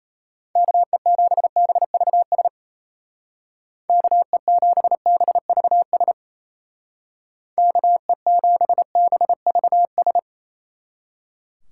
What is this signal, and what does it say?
Morse code, heard as a keyed tone, about 34 wpm, 700 Hz. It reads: KE7BVS KE7BVS KE7BVS
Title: Speed Racing - USA Call Signs 28wpm